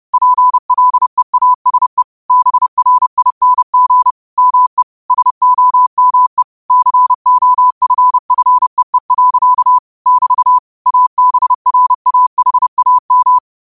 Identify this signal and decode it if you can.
Morse code, heard as a keyed tone, about 30 wpm, 1000 Hz. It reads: PLEASE BRING ME SOME COFFEE. <BT> ABRAHAM